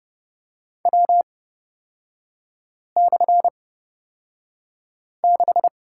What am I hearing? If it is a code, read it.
Morse code, heard as a keyed tone, about 30 wpm, spaced as for 5 wpm, 700 Hz. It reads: W/6
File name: Lesson 37e - Letter x - Three Characters 30wpm